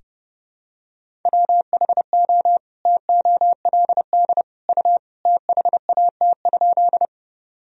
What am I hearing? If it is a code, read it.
Morse code, heard as a keyed tone, about 31 wpm, 700 Hz. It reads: WHO TOLD U THAT?
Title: Sentences from Top 400 Words and Abbreviations 30wpm